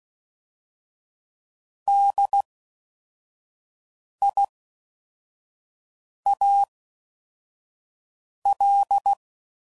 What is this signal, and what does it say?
Morse code, heard as a keyed tone, about 16 wpm, spaced as for 4 wpm, 800 Hz. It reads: DIAL